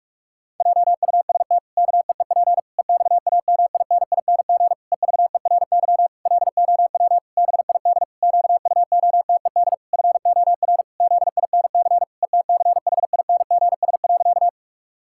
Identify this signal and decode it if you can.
Morse code, heard as a keyed tone, about 46 wpm, 700 Hz. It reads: JUST KEEP EXAMINING EVERY LOW BID QUOTED FOR ZINC ETCHINGS.
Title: Pangram Challenge v2 - 45wpm